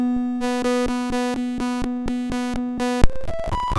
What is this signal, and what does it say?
There is no code.